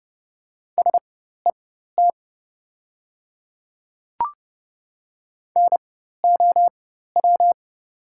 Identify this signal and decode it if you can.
Morse code, heard as a keyed tone, about 30 wpm, spaced as for 14 wpm, 700 Hz. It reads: SET NOW
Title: 4x Character Spacing - Top 200 Words - 30wpm